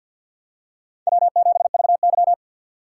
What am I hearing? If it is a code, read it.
Morse code, heard as a keyed tone, about 51 wpm, 700 Hz. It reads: W7VY
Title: USA Call Signs - 4 Characters - 2x Character Spacing - 50wpm